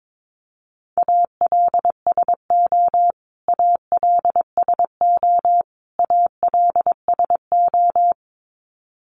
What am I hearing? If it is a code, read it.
Morse code, heard as a keyed tone, about 22 wpm, 700 Hz. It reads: ALSO ALSO ALSO
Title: Top 300 Words Repeated 22wpm